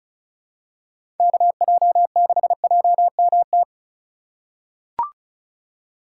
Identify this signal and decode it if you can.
Morse code, heard as a keyed tone, about 34 wpm, 700 Hz. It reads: KJ6JMT
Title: USA Call Signs - 6 Characters - 2x Character Spacing - 35wpm